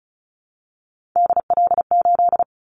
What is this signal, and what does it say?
Morse code, heard as a keyed tone, about 36 wpm, 690 Hz. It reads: DL8
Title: Callsign Prefixes for Top 50 DXCC - 35wpm